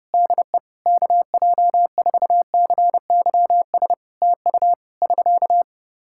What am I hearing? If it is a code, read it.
Morse code, heard as a keyed tone, about 30 wpm, 700 Hz. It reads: DE KJ4CYS TU <SK>
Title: Practice QSO - Exchange 4 - Ending - 30wpm